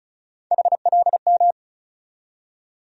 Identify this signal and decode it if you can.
Morse code, heard as a keyed tone, about 34 wpm, 700 Hz. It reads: HLM
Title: USA Call Sign Postfixes - 3 Letters 35wpm